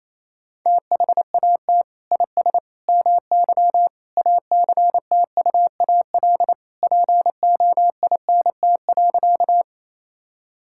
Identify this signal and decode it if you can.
Morse code, heard as a keyed tone, about 28 wpm, 700 Hz. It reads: THAT IS MY ACTUAL POINT.
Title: Sentences from Top 1000 Words - Review Part 5 - 28wpm